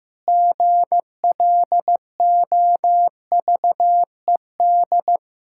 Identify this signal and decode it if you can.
Morse code, heard as a keyed tone, about 15 wpm, 700 Hz. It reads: GLOVED
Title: Pangram Challenge v2 - 15wpm